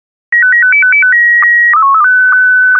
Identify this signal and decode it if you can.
SSTV (slow-scan television), Scottie 1.